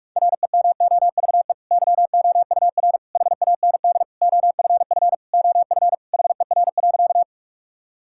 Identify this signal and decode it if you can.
Morse code, heard as a keyed tone, about 46 wpm, 700 Hz. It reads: REMOVE YOUR HAND OFF OF HER.
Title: Sentences from Top 1000 Words - Review Part 5 - 45wpm